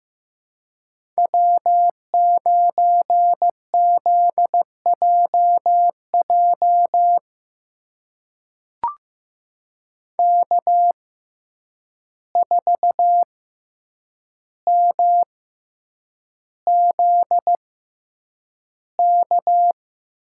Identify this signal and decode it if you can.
Morse code, heard as a keyed tone, about 15 wpm, spaced as for 8 wpm, 700 Hz. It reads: W9ZJJ K4MZK
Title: USA Call Signs - 5 Characters - 6x Character Spacing - 15wpm